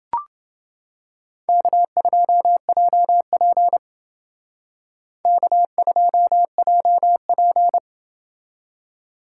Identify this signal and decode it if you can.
Morse code, heard as a keyed tone, about 29 wpm, 700 Hz. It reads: K2JP K2JP